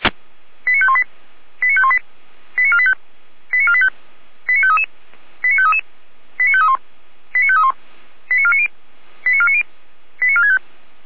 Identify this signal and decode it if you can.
Five-tone selective call (ZVEI), heard as quick runs of five tones, tones of about 70 ms. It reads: 89518 89518 89485 89485 89533 89533 89531 89531 89490 89490 89465